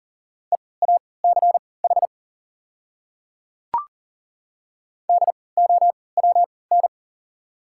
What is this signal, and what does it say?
Morse code, heard as a keyed tone, about 40 wpm, spaced as for 23 wpm, 690 Hz. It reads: EACH DOWN
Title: 3x Character Spacing - Top 100 Words - 40wpm